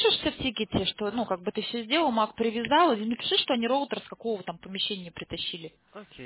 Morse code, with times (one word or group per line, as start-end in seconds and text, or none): none